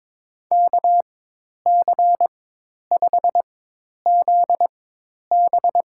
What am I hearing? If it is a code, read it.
Morse code, heard as a keyed tone, about 22 wpm, spaced as for 10 wpm, 700 Hz. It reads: KC5ZB
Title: USA Call Signs - 4x Character Spacing - 22wpm